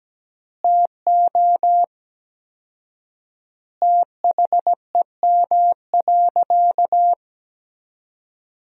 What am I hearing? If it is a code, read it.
Morse code, heard as a keyed tone, about 17 wpm, 700 Hz. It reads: TO THEM.
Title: 4x Word-Spacing - Sentences from Top 500 Words - 17wpm